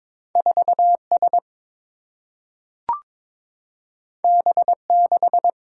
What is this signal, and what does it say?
Morse code, heard as a keyed tone, about 21 wpm, 700 Hz. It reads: 4 S B 6
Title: Single-Letter Single-Number 22wpm